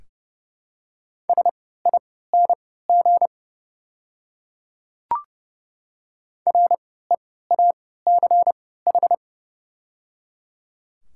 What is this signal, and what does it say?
Morse code, heard as a keyed tone, about 30 wpm, spaced as for 17 wpm, 700 Hz. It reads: SING REACH
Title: 3x Character Spacing - Top 500 Words - 30wpm